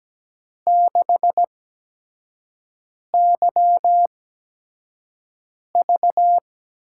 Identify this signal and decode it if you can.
Morse code, heard as a keyed tone, about 17 wpm, spaced as for 5 wpm, 700 Hz. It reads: 6YV
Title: USA Call Signs - 8x Character Spacing - 17wpm